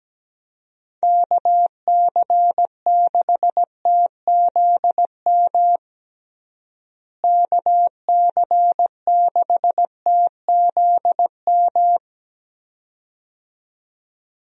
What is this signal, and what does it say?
Morse code, heard as a keyed tone, about 17 wpm, 700 Hz. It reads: KC6TZM KC6TZM
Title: USA Call Signs - 6 Characters - Repeated 2x - 17wpm